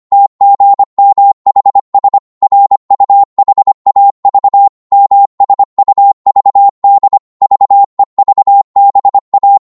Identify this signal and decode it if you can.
Morse code, heard as a keyed tone, about 25 wpm, 800 Hz. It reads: TGMHS RUHAV MSUVD VEVBA